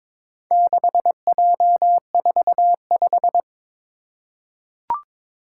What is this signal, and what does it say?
Morse code, heard as a keyed tone, about 22 wpm, 700 Hz. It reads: 6J45
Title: Four Challenging Letter-Number Combinations 22wpm